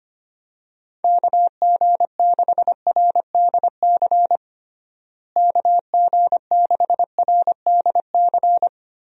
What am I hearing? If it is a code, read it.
Morse code, heard as a keyed tone, about 25 wpm, 700 Hz. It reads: KG6RDC KG6RDC
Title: USA Call Signs - 6 Characters - Repeated 2x - 25wpm